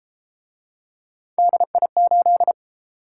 Morse code, heard as a keyed tone, about 31 wpm, 700 Hz. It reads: DI8